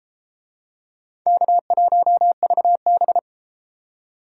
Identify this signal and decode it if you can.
Morse code, heard as a keyed tone, about 29 wpm, 700 Hz. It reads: K1VB